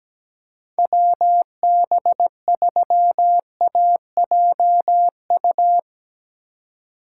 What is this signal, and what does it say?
Morse code, heard as a keyed tone, about 17 wpm, 700 Hz. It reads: WB3AJU
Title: Speed Racing - USA 2x3 Call Signs 17wpm